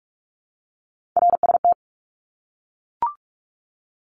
Morse code, heard as a keyed tone, about 44 wpm, 700 Hz. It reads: RST
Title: Abbreviations 45wpm